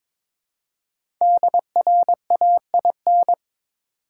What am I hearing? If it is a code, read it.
Morse code, heard as a keyed tone, about 23 wpm, 700 Hz. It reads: DRAIN
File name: 5 Letter Words - First 3 Letters Given - 22wpm